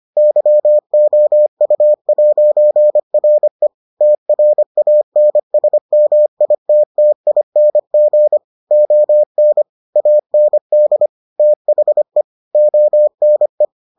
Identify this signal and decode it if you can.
Morse code, heard as a keyed tone, about 25 wpm, 600 Hz. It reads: YOU'RE TRANSMITTING ON AND THE ONE